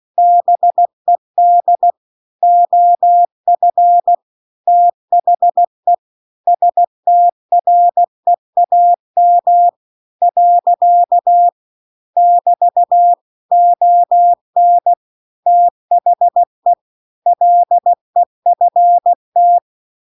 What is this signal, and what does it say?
Morse code, heard as a keyed tone, about 16 wpm, 700 Hz. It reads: BED OF THE STREAM . <BT> ON THE LEFT
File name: CW-The-Big-Twoheaded-River-2-16WPM